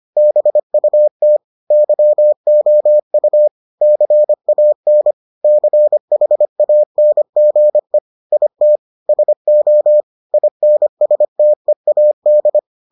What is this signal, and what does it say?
Morse code, heard as a keyed tone, about 25 wpm, 600 Hz. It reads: BUT YOU CAN CHANGE IT SO INSTEAD